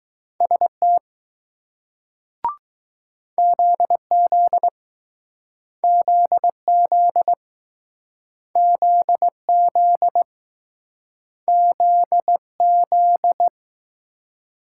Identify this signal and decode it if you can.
Morse code, heard as a keyed tone, about 20 wpm, 700 Hz. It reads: ST ZZ ZZ ZZ Z Z